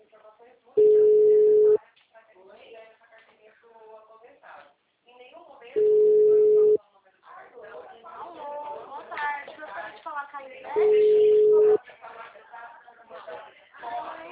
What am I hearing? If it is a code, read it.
Ringback tone (Europe, 425 Hz), with speech.
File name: out- -2025- - - .69492